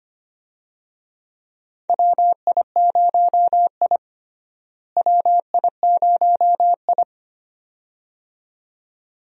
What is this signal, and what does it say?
Morse code, heard as a keyed tone, about 25 wpm, 700 Hz. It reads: WI0I WI0I
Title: USA Call Signs - 4 Characters - Repeated 2x - 25wpm